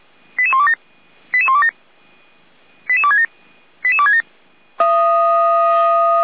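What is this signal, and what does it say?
Five-tone selective call (ZVEI), heard as quick runs of five tones, tones of about 70 ms. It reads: 88127 88127 88267 88267
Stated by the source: Five-tone ZVEI selective call received by SDR, logged by Feuerwehr Orlamünde.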